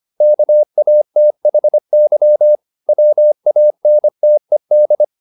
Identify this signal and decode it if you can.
Morse code, heard as a keyed tone, about 25 wpm, 600 Hz. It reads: KATHY WANTED